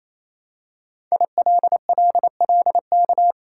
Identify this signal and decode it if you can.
Morse code, heard as a keyed tone, about 27 wpm, 700 Hz. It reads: ILLLK
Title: Code Groups - 5 Letters - 28wpm